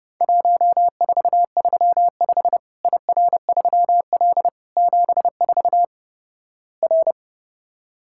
Morse code, heard as a keyed tone, about 32 wpm, 700 Hz. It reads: 1435 IR3L 74 R
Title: Worked All Europe DX Contest - QTC Exchange - 30wpm